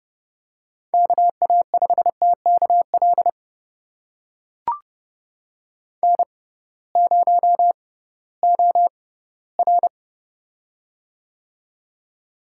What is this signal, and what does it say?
Morse code, heard as a keyed tone, about 32 wpm, spaced as for 18 wpm, 700 Hz. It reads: KA5TKL N0OR
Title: USA Call Signs - 6x Character Spacing - 30wpm